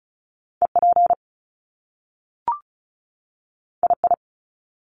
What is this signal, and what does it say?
Morse code, heard as a keyed tone, about 36 wpm, 700 Hz. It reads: EP II
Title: Top Two Letter Combinations in Words 35wpm